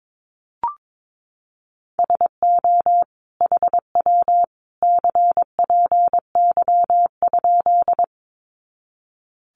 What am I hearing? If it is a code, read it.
Morse code, heard as a keyed tone, about 22 wpm, 700 Hz. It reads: SO HW CPY?
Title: Common QSO Phrases 22wpm